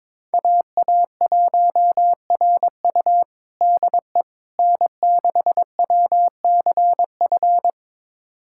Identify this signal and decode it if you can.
Morse code, heard as a keyed tone, about 22 wpm, 700 Hz. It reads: AA1RU DE N6WCF
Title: USA Callsign DE Callsign - 5 Characters - 22wpm